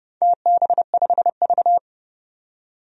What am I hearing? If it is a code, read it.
Morse code, heard as a keyed tone, about 32 wpm, 700 Hz. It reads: TB5V
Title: DXCC-100 Call Signs - 4 Characters - 30wpm